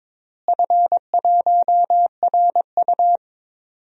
Morse code, heard as a keyed tone, about 22 wpm, 700 Hz. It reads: F1RU
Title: DXCC-100 Call Signs - 4 Characters - Repeated 3x - 22wpm